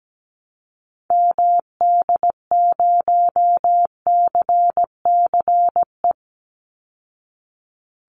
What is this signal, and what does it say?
Morse code, heard as a keyed tone, about 17 wpm, 700 Hz. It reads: MD0CCE
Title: Call Signs Repeated 2x - 17wpm